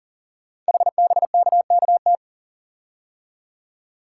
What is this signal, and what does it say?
Morse code, heard as a keyed tone, about 40 wpm, 700 Hz. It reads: HBKKT
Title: Code Groups - 5 Letters - 40wpm